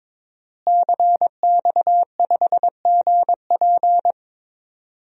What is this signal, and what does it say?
Morse code, heard as a keyed tone, about 21 wpm, 700 Hz. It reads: CX5GP
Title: DXCC-100 Call Signs - 5 Characters - Repeated 2x - 22wpm